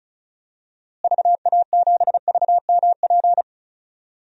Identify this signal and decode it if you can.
Morse code, heard as a keyed tone, about 36 wpm, 700 Hz. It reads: VA7VMP
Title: DXCC-100 Call Signs - 6 Characters - Repeated 2x - 35wpm